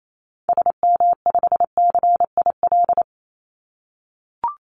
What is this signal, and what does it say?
Morse code, heard as a keyed tone, about 28 wpm, 700 Hz. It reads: SM5CIL
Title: Call Signs - 28wpm